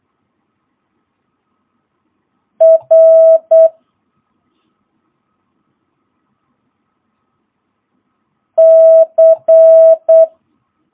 Morse code, heard as a keyed tone, about 8 wpm, 650 Hz. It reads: R C
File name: jh8cbh / 2022cw